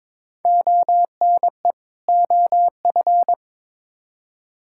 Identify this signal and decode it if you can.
Morse code, heard as a keyed tone, about 22 wpm, 700 Hz. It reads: ONE OF
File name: Build A Phrase - 2-Word Phrases - 22wpm